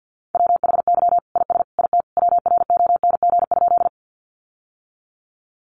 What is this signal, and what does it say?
Morse code, heard as a keyed tone, about 50 wpm, 700 Hz. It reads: WHY IS IT WRONG?
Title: Sentences from Top 900 Words 50wpm